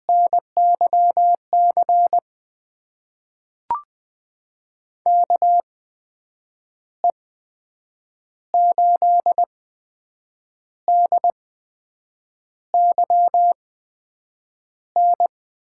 Morse code, heard as a keyed tone, about 20 wpm, spaced as for 7 wpm, 700 Hz. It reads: NYC KE8DYN
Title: USA Call Signs - 6 Characters - 8x Character Spacing - 20wpm